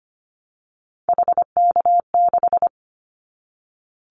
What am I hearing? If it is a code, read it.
Morse code, heard as a keyed tone, about 25 wpm, 700 Hz. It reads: HK6
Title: Callsign Prefixes for Top 100 DXCC - 25wpm